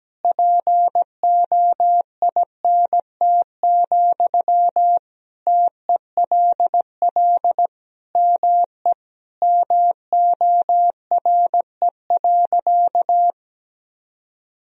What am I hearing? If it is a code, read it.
Morse code, heard as a keyed tone, about 17 wpm, 700 Hz. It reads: POINT, TELL ME MORE.
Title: Sentences from Top 1000 Words - Review Part 1 - 17wpm